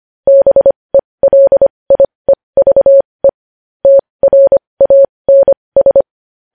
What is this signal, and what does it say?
Morse code, heard as a keyed tone, about 25 wpm, spaced as for 20 wpm, 550 Hz. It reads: BELIEVE TRANS
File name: podcast / News headlines-1x-25x20